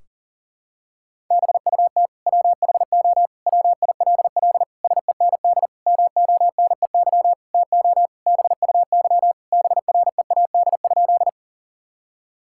Morse code, heard as a keyed tone, about 40 wpm, 700 Hz. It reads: BUT WHO WILL SEND MONEY TO BUY BREAD?
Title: Sentences from Top 1600 Words 40wpm